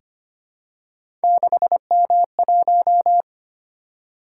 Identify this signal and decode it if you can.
Morse code, heard as a keyed tone, about 25 wpm, 700 Hz. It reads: 6M1
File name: Three Letter-Number Combinations - 25wpm